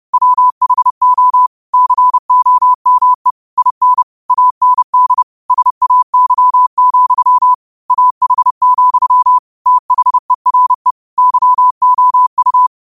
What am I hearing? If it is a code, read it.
Morse code, heard as a keyed tone, about 30 wpm, 1000 Hz. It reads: WHO COME IN AND SAY, AH, THERE YOU
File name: qotdfiles / 30wpm